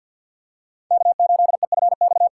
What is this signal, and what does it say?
Morse code, heard as a keyed tone, about 51 wpm, 700 Hz. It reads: K8EFX
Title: USA Callsign DE Callsign - 5 Characters - 50wpm